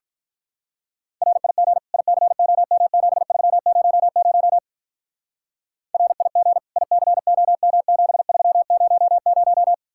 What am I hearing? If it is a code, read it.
Morse code, heard as a keyed tone, about 46 wpm, 700 Hz. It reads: RIG ICOM7300 RIG ICOM7300